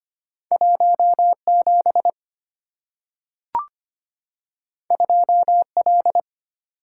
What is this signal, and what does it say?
Morse code, heard as a keyed tone, about 25 wpm, 700 Hz. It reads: 17 2L
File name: Two Letter-Number Combinations - 25wpm